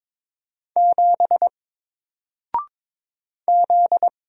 Morse code, heard as a keyed tone, about 22 wpm, 700 Hz. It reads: 7 Z
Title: Single Character (Letter:Number:Punctuation:Prosign) Alt 22wpm